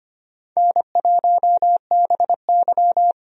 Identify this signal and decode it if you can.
Morse code, heard as a keyed tone, about 25 wpm, 700 Hz. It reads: N1BY
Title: USA Call Signs - 4 Characters - Repeated 2x - 25wpm